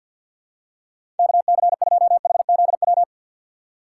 Morse code, heard as a keyed tone, about 50 wpm, 700 Hz. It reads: KC2HZW
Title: USA Call Signs - 6 Characters - 3x Character Spacing - 50wpm